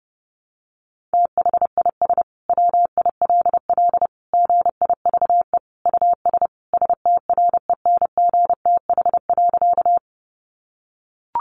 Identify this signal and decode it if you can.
Morse code, heard as a keyed tone, about 30 wpm, 700 Hz. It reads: THIS WILL GIVE US STRENGTH.
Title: Sentences from Top 1300 Words 30wpm